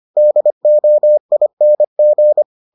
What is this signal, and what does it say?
Morse code, heard as a keyed tone, about 25 wpm, 600 Hz. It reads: DOING